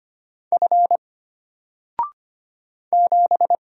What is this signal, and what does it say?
Morse code, heard as a keyed tone, about 25 wpm, 700 Hz. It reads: F 7
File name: Single-Letter-Number-25wpm